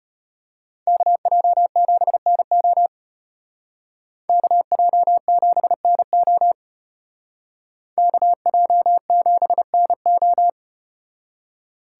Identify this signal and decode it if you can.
Morse code, heard as a keyed tone, about 34 wpm, 700 Hz. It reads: KJ7NO KJ7NO KJ7NO